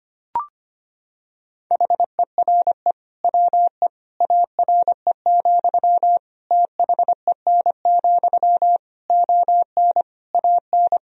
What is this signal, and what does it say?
Morse code, heard as a keyed tone, about 25 wpm, 700 Hz. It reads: HERE WE ARE, THEN, ON AN
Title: Sentences from Top 1000 Words - Review Part 3 - 25wpm